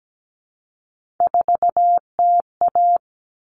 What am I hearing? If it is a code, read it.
Morse code, heard as a keyed tone, about 17 wpm, 700 Hz. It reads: 4TA
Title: Common Cut Numbers - 3-Digit 17wpm